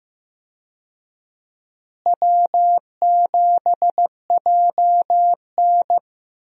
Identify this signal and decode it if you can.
Morse code, heard as a keyed tone, about 15 wpm, 700 Hz. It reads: W7JN
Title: USA Call Signs - 4 Characters - Repeated 3x - 15wpm